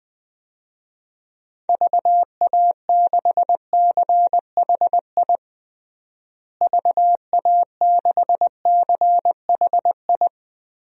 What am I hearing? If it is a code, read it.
Morse code, heard as a keyed tone, about 20 wpm, 700 Hz. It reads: VA6CHI VA6CHI
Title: DXCC-100 Call Signs - 6 Characters - Repeated 2x - 20wpm